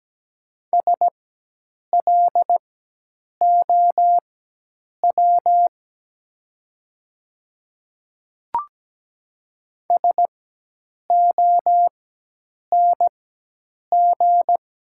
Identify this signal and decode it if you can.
Morse code, heard as a keyed tone, about 17 wpm, spaced as for 8 wpm, 700 Hz. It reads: SLOW SONG
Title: 4x Character Spacing - Top 500 Words - 17wpm